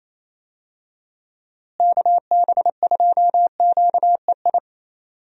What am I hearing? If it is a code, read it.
Morse code, heard as a keyed tone, about 28 wpm, 700 Hz. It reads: KB2QEI
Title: USA Call Signs - 6 Characters - Repeated 2x - 28wpm